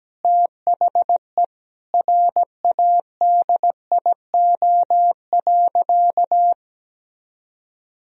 Morse code, heard as a keyed tone, about 17 wpm, 700 Hz. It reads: THE RADIO.
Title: Sentences from Top 1000 Words - Review Part 3 - 17wpm